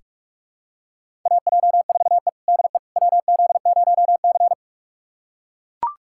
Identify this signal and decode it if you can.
Morse code, heard as a keyed tone, about 42 wpm, 700 Hz. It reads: AJ4E DE WZ0C
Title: USA Callsign DE Callsign - 4 Characters - 45wpm